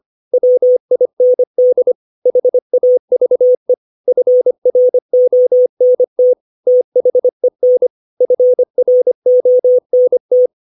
Morse code, heard as a keyed tone, about 25 wpm, 500 Hz. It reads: WIND HAVE FRONT THEN FRONT
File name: 632 words / 632 Words 25WPM